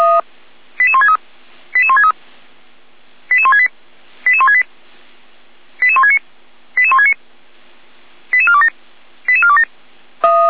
Five-tone selective call (ZVEI), heard as quick runs of five tones, tones of about 70 ms. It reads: 88163 88163 88168 88168 88169 88169 88427 88427
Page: SDR/ZVEI